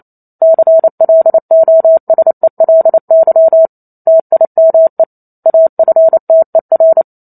Morse code, heard as a keyed tone, about 28 wpm, 650 Hz. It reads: CLOSELY TIME AFTER
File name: Chapter 2 - 28 WPM